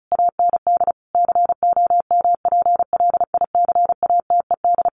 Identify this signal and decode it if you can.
Morse code, heard as a keyed tone, about 34 wpm, 700 Hz. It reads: AND COMPLICATED
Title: Sentences from Top 1200 Words 35wpm